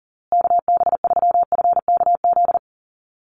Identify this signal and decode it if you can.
Morse code, heard as a keyed tone, about 40 wpm, 700 Hz. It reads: KB3FKZ
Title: USA Call Signs Repeated 3x 40wpm